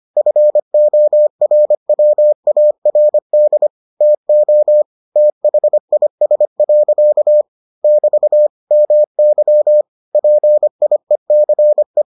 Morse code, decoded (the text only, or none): FORWARD TO THIS. <BT> MY PIECE